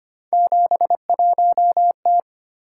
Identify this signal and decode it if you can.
Morse code, heard as a keyed tone, about 25 wpm, 700 Hz. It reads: 71T
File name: Three Letter-Number Combinations - 25wpm